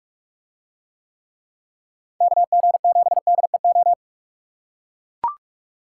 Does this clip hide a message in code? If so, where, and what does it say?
Yes: Morse code from 2.2 to 3.9 s, heard as a keyed tone, about 43 wpm, 700 Hz. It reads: KG7DEO